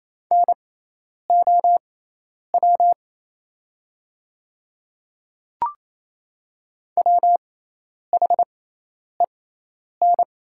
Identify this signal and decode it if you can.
Morse code, heard as a keyed tone, about 28 wpm, spaced as for 10 wpm, 700 Hz. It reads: NOW WHEN
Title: 6x Character Spacing - Top 100 Words - 28wpm